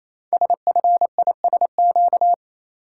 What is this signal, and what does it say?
Morse code, heard as a keyed tone, about 29 wpm, 700 Hz. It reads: SFISQ
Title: Code Groups - 5 Letter:Numbers - 28wpm